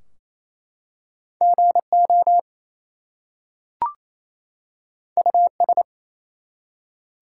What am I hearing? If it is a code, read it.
Morse code, heard as a keyed tone, about 27 wpm, 700 Hz. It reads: GO US
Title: Two Letter Words - 28wpm